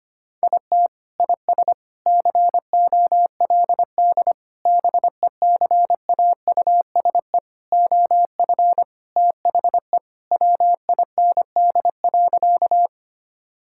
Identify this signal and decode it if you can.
Morse code, heard as a keyed tone, about 25 wpm, 700 Hz. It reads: IT IS COLD BECAUSE OF THE WIND.
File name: practice / 25wpm / Sentences with 3-Word Phrases - 25wpm